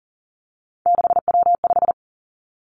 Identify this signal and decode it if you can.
Morse code, heard as a keyed tone, about 41 wpm, 700 Hz. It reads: 6W5
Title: Three Challenging Letter-Number Combinations 40wpm